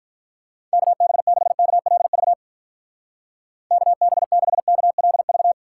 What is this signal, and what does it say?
Morse code, heard as a keyed tone, about 45 wpm, 700 Hz. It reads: KB6KLV KB6KLV